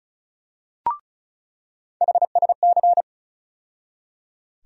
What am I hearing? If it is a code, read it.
Morse code, heard as a keyed tone, about 36 wpm, 700 Hz. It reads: HSC
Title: USA Call Sign Postfixes - 3 Letters 35wpm